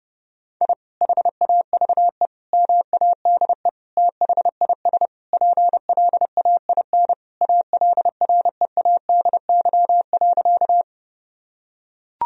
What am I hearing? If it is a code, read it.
Morse code, heard as a keyed tone, about 32 wpm, 700 Hz. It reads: I HAVE MADE THIS PLAIN ALREADY.
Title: Sentences from Top 1000 Words - Review Part 2 - 30wpm